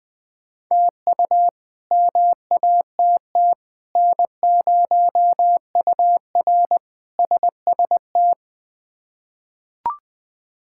Morse code, heard as a keyed tone, about 20 wpm, 700 Hz. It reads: TU MATT N0UR SST
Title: K1USN SST QSO Practice - 20wpm